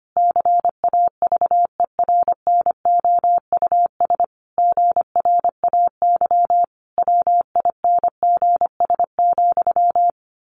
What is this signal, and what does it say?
Morse code, heard as a keyed tone, about 25 wpm, 700 Hz. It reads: CAVERNOUS GRAY WINGS,
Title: A Sound Of Thunder 25wpm